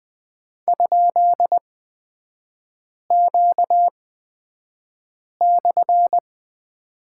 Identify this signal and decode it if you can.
Morse code, heard as a keyed tone, about 20 wpm, spaced as for 5 wpm, 700 Hz. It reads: ?Q/
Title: 250 Lesson 40e - BK - Three Characters 20wpm